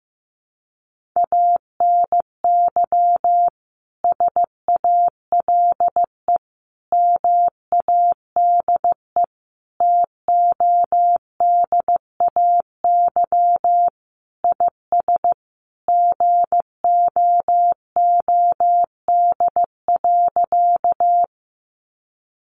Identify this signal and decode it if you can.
Morse code, heard as a keyed tone, about 15 wpm, 700 Hz. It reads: ANY SALE MADE TODAY IS GOOD.
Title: Sentences from Top 1400 Words 15wpm